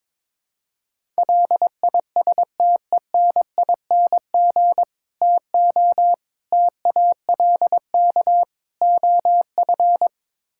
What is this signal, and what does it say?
Morse code, heard as a keyed tone, about 21 wpm, 700 Hz. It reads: LISTENING TO TALK OF
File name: practice / 22wpm / Build A Phrase - 5-Word Phrases - 22wpm